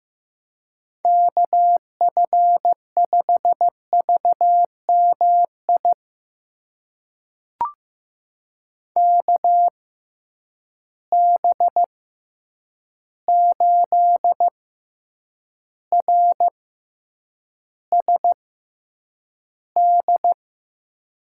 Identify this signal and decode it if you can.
Morse code, heard as a keyed tone, about 15 wpm, spaced as for 8 wpm, 700 Hz. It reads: KF5VMI KB8RSD
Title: USA Call Signs - 6 Characters - 6x Character Spacing - 15wpm